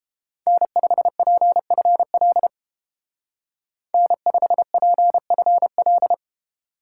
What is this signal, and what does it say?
Morse code, heard as a keyed tone, about 29 wpm, 700 Hz. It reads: N5PFL N5PFL